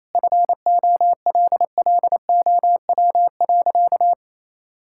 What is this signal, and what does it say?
Morse code, heard as a keyed tone, about 28 wpm, 700 Hz. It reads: FOLLOW.
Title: Sentences from Top 1000 Words - Review Part 6 - 28wpm